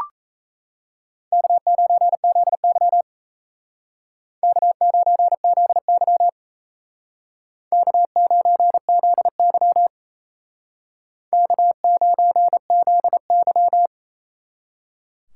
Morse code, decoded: K9ZY K9ZY K9ZY K9ZY

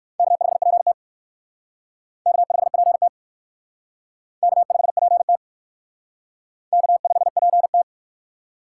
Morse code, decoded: K5PT K5PT K5PT K5PT

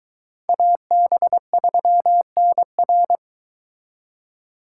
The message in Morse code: AB3NR

